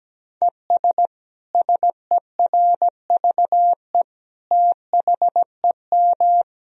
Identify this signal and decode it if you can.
Morse code, heard as a keyed tone, about 17 wpm, 700 Hz. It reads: ES SERVE THEM
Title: Sentences from Top 400 Words and Abbreviations 17wpm